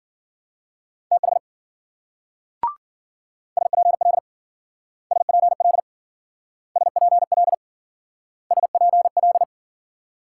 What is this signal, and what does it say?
Morse code, heard as a keyed tone, about 52 wpm, 700 Hz. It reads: TH SPL SPL SPL SPL